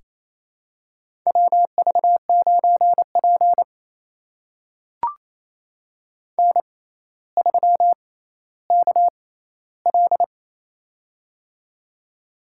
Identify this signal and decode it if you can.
Morse code, heard as a keyed tone, about 27 wpm, spaced as for 14 wpm, 700 Hz. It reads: WV9P N3KL
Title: USA Call Signs - 4 Characters - 6x Character Spacing - 28wpm